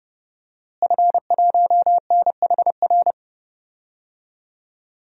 Morse code, heard as a keyed tone, about 30 wpm, 700 Hz. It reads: F1NHR
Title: DXCC-100 Call Signs - 5 Characters - Repeated 2x - 30wpm